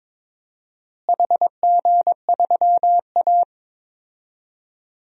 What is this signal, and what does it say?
Morse code, heard as a keyed tone, about 22 wpm, 700 Hz. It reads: HG3A
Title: DXCC-100 Call Signs - 4 Characters - Repeated 3x - 22wpm